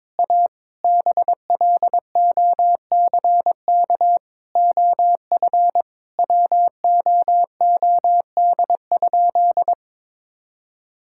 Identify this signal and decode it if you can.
Morse code, heard as a keyed tone, about 22 wpm, 700 Hz. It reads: A BLOCK OF WOOD?
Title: Sentences from Top 1600 Words 22wpm